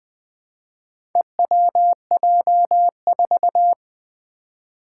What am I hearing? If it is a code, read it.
Morse code, heard as a keyed tone, about 20 wpm, 700 Hz. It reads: EWJ4
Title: Four Challenging Letter-Number Combinations 20wpm